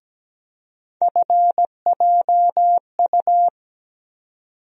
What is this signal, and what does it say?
Morse code, heard as a keyed tone, about 17 wpm, 700 Hz. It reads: FJU
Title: USA Call Sign Postfixes - 3 Letters 17wpm